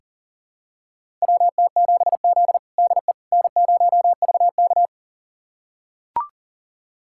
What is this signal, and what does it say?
Morse code, heard as a keyed tone, about 40 wpm, 700 Hz. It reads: WT7Z DE N0VK
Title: USA Callsign DE Callsign - 4 Characters - 40wpm